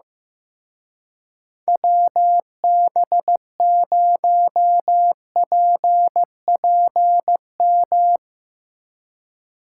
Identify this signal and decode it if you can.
Morse code, heard as a keyed tone, about 15 wpm, 700 Hz. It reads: WB0PPM
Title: Speed Racing - USA Call Signs 15wpm